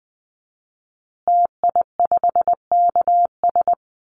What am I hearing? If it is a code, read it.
Morse code, heard as a keyed tone, about 19 wpm, 700 Hz. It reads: TI5KS
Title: Call Signs Repeated 2x - 20wpm